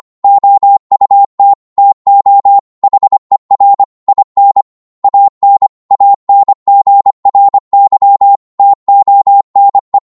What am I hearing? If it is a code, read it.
Morse code, heard as a keyed tone, about 25 wpm, 800 Hz. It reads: OUT TO HER IN AN ANGRY TONE
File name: 25wpm